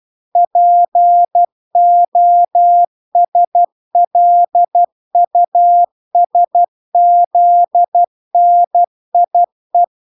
Morse code, heard as a keyed tone, about 12 wpm, 700 Hz. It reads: POSLUSZNIE